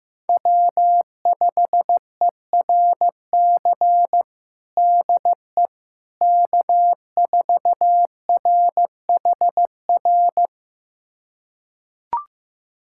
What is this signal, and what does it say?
Morse code, heard as a keyed tone, about 15 wpm, 700 Hz. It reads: W5ERC DE K4RHR
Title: USA Callsign DE Callsign 15wpm